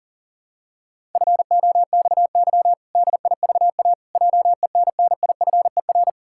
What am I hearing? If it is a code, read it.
Morse code, heard as a keyed tone, about 37 wpm, 700 Hz. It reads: FOXY DIVA JENNIFER
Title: Pangram Challenge v2 - 40wpm